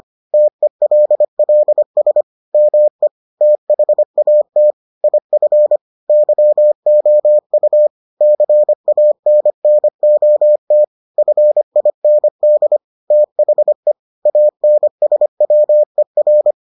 Morse code, heard as a keyed tone, about 25 wpm, 600 Hz. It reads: TELLS ME THAT IF YOU CANNOT FIND THE ANSWER